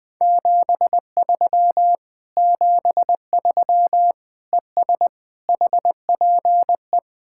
Morse code, heard as a keyed tone, about 20 wpm, 700 Hz. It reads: 73 73 ES HPE
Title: Speed-Racing Common QSO Phrases - 20wpm